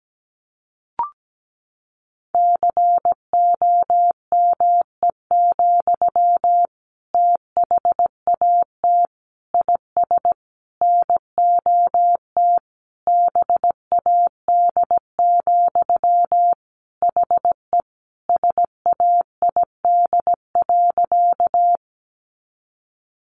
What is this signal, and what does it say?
Morse code, heard as a keyed tone, about 17 wpm, 700 Hz. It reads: COME, THAT IS NOT BAD, HE SAID.
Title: Sentences from Top 700 Words 17wpm